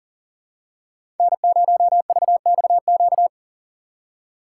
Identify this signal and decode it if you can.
Morse code, heard as a keyed tone, about 40 wpm, 700 Hz. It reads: N0VXQ
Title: USA Call Signs - 5 Characters - 2x Character Spacing - 40wpm